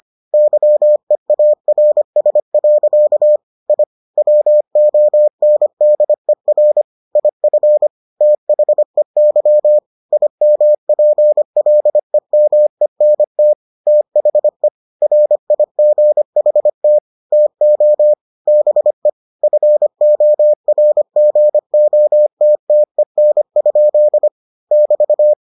Morse code, heard as a keyed tone, about 25 wpm, 600 Hz. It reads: YEARS. I WONDER IF THEY IMPLEMENT THE RIGHT TO BE FORGOTTEN? <BT>